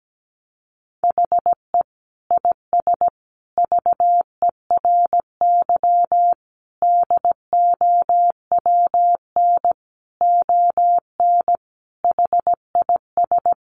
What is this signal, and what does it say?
Morse code, heard as a keyed tone, about 17 wpm, 700 Hz. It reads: HE IS VERY DOWN ON HIS
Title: Sentences from Top 1500 Words 17wpm